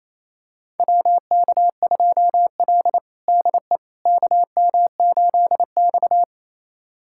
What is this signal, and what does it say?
Morse code, heard as a keyed tone, about 28 wpm, 700 Hz. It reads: WK2L DE KM8X
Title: USA Callsign DE Callsign - 4 Characters - 28wpm